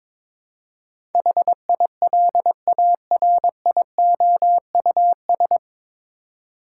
Morse code, heard as a keyed tone, about 22 wpm, 700 Hz. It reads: HILARIOUS
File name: 9 Letter Words - First 3 Letters Given - 22wpm